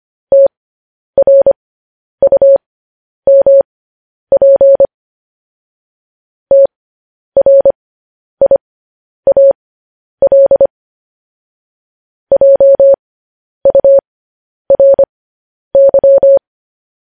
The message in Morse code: TRUMP TRIAL JURY

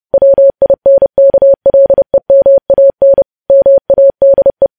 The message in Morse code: WINKLEMAN MADE

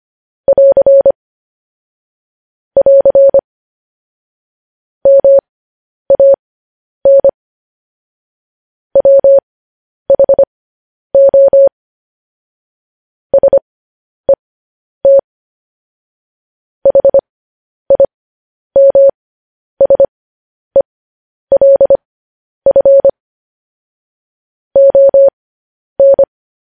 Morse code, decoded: <AR> <AR> MAN WHO SET HIMSELF ON

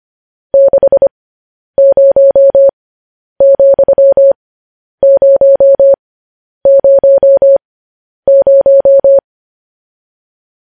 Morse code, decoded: 60,000

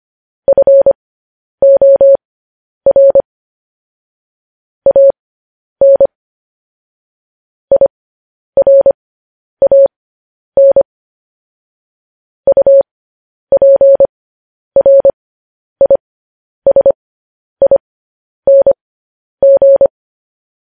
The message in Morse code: FOR AN IRAN UPRISING

